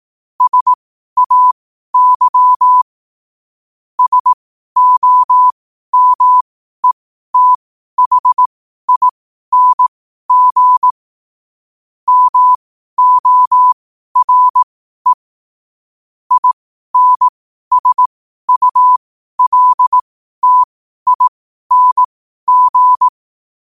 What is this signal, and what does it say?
Morse code, heard as a keyed tone, about 18 wpm, spaced as for 12 wpm, 1000 Hz. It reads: SAY SOMETHING MORE INSULTING
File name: qotdfiles / 12wpm